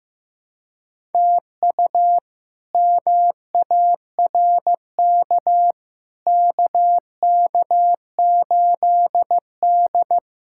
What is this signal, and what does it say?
Morse code, heard as a keyed tone, about 15 wpm, 700 Hz. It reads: TU MARK KK8D
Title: K1USN SST QSO Practice - 15wpm